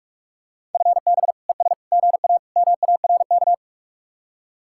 Morse code, heard as a keyed tone, about 49 wpm, 700 Hz. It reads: FB ES GA MARK